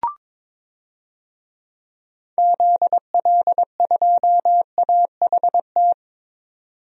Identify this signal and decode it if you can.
Morse code, heard as a keyed tone, about 21 wpm, 700 Hz. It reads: ZL2AHT
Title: DXCC-100 Call Signs - 6 Characters - 22wpm